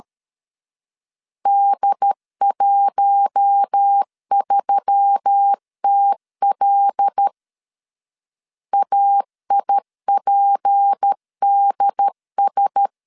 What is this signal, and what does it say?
Morse code, heard as a keyed tone, about 13 wpm, 800 Hz. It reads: D13TL AIPDS
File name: CW / DET / 1 6 1500C S B 12wpm 3431